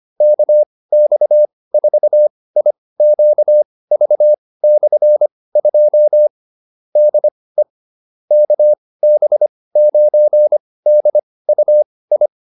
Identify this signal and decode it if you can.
Morse code, heard as a keyed tone, about 25 wpm, spaced as for 18 wpm, 600 Hz. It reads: KX4IQV/2 DE KB9DUI
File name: QSO308 18